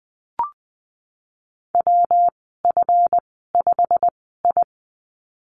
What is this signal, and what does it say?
Morse code, heard as a keyed tone, about 20 wpm, spaced as for 15 wpm, 700 Hz. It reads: WF5I
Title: 147 Lesson 24g - Letter y - Call Signs 20wpm